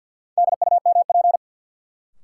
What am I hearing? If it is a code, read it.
Morse code, heard as a keyed tone, about 53 wpm, 700 Hz. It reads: DUMP
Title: Sentences from Top 1300 Words - New Word Drill - 50wpm